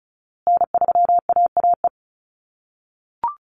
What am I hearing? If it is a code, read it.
Morse code, heard as a keyed tone, about 34 wpm, 700 Hz. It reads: N3AAE
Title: USA Call Signs Repeated 3x 35wpm